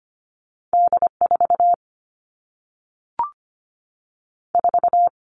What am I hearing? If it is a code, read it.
Morse code, heard as a keyed tone, about 25 wpm, 700 Hz. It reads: D4 4
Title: All Cut Numbers - 2-Digit 25wpm